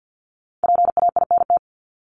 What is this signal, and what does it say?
Morse code, heard as a keyed tone, about 50 wpm, 700 Hz. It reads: PAINT
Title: Top 751-1000 Words 50wpm